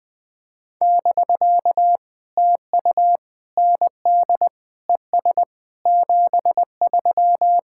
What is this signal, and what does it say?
Morse code, heard as a keyed tone, about 20 wpm, 700 Hz. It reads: <BK> TU ND ES 73
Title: POTA QSO Practice - 20wpm